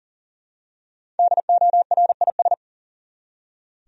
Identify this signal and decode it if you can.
Morse code, heard as a keyed tone, about 40 wpm, 700 Hz. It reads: DORIS
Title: USA Common Names - Max 5 Letters - 40wpm